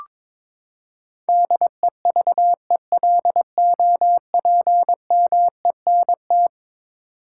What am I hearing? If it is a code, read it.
Morse code, heard as a keyed tone, about 22 wpm, 700 Hz. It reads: DEVELOPMENT
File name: practice / 22wpm / Eleven Letter Words - 22wpm